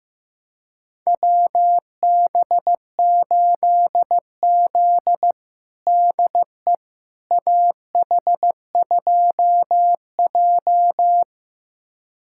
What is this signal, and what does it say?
Morse code, heard as a keyed tone, about 15 wpm, 700 Hz. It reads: WB8Z DE AH2J
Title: USA Callsign DE Callsign - 4 Characters - 15wpm